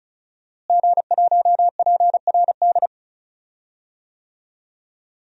Morse code, heard as a keyed tone, about 34 wpm, 700 Hz. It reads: G1PRD
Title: DXCC-100 Call Signs - 5 Characters - Repeated 2x - 35wpm